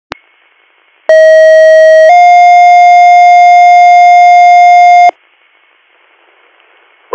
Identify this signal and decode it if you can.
Two-tone page, heard as a one-second tone, then a three-second tone, about 640 then 710 Hz. It is heard from 1.1 to 5.1 s.